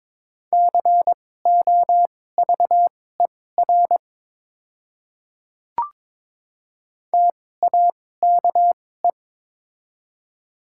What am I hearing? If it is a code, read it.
Morse code, heard as a keyed tone, about 22 wpm, spaced as for 16 wpm, 700 Hz. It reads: COVER TAKE
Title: 2x Character Spacing - Top 500 Words - 22wpm